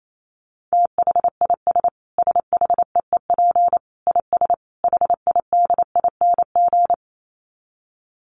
Morse code, heard as a keyed tone, about 28 wpm, 700 Hz. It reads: THIS SHEEP IS HIDING
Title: Dit Challenge 28wpm